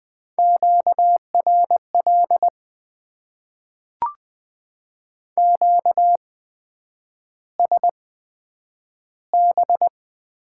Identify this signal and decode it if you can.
Morse code, heard as a keyed tone, about 20 wpm, spaced as for 9 wpm, 700 Hz. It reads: QRL QSB
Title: Top 90 QSO Elements - 8x Character Spacing - 20wpm